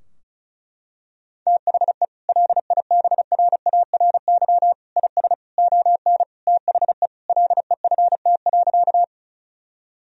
Morse code, heard as a keyed tone, about 36 wpm, 700 Hz. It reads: THE LIBRARY IS ON THE LEFT.
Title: Sentences from Top 1000 Words - Review Part 4 - 35wpm